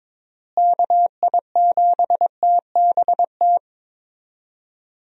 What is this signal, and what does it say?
Morse code, heard as a keyed tone, about 22 wpm, 700 Hz. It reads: KI7TBT
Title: USA Call Signs - 6 Characters - Repeated 3x - 22wpm